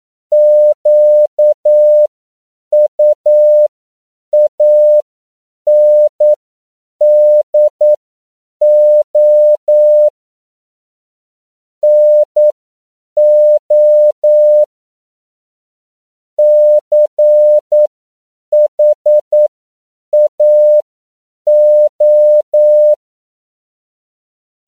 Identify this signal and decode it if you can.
Morse code, heard as a keyed tone, about 9 wpm, spaced as for 7 wpm, 600 Hz. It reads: QUANDO NO CHAO